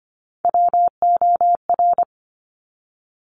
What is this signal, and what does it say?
Morse code, heard as a keyed tone, about 25 wpm, 700 Hz. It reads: WOR
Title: Top Three Letter Combinations in Words 25wpm